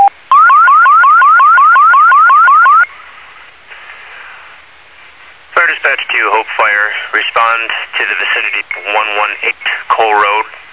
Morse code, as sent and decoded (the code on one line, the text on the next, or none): none